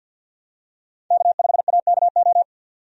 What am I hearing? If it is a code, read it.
Morse code, heard as a keyed tone, about 49 wpm, 700 Hz. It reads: K5AKO
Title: USA Call Signs - 5 Characters - 50wpm